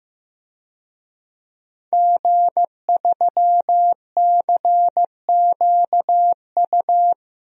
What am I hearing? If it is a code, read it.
Morse code, heard as a keyed tone, about 16 wpm, 700 Hz. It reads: G3CQU
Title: DXCC-100 Call Signs - 5 Characters - Repeated 2x - 15wpm